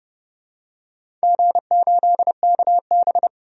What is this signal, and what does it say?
Morse code, heard as a keyed tone, about 29 wpm, 700 Hz. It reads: G8KB
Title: DXCC-100 Call Signs - 4 Characters - 30wpm